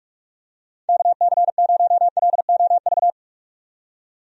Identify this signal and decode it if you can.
Morse code, heard as a keyed tone, about 46 wpm, 700 Hz. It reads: KC0LOV
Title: USA Call Signs - 6 Characters - 45wpm